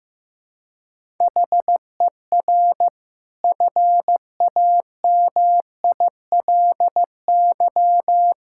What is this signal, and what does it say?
Morse code, heard as a keyed tone, about 15 wpm, 700 Hz. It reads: HER FAMILY
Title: Sentences from Top 1000 Words - Review Part 5 - 15wpm